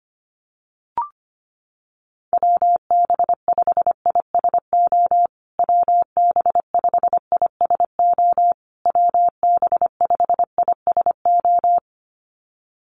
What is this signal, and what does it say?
Morse code, heard as a keyed tone, about 25 wpm, 700 Hz. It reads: WB5ISO WB5ISO WB5ISO
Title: USA Call Signs Repeated 3x 25wpm